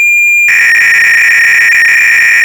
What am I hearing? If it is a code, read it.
300-baud modem data.